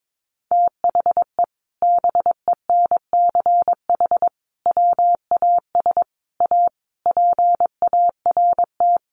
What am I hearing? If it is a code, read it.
Morse code, heard as a keyed tone, about 22 wpm, 700 Hz. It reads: THE BENCH WAS A PART
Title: Sentences from Top 1400 Words 22wpm